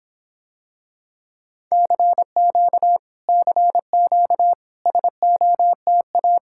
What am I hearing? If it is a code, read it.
Morse code, heard as a keyed tone, about 26 wpm, 700 Hz. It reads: CQ CQ SOTA